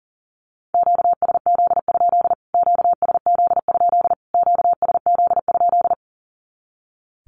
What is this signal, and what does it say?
Morse code, heard as a keyed tone, about 40 wpm, 700 Hz. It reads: QSZ? QSZ? QSZ?
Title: Q-Codes Repeated 3x 40wpm